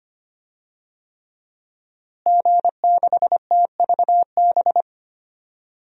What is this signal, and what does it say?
Morse code, heard as a keyed tone, about 25 wpm, 700 Hz. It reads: G6TVB
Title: DXCC-100 Call Signs - 5 Characters - Repeated 3x - 25wpm